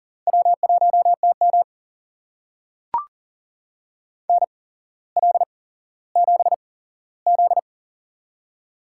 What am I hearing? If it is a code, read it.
Morse code, heard as a keyed tone, about 40 wpm, spaced as for 17 wpm, 700 Hz. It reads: W1TM NL7Z
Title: USA Call Signs - 4 Characters - 8x Character Spacing - 40wpm